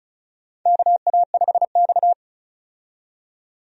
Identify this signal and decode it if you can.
Morse code, heard as a keyed tone, about 34 wpm, 700 Hz. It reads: KA5X
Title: USA Call Signs - 4 Characters - 35wpm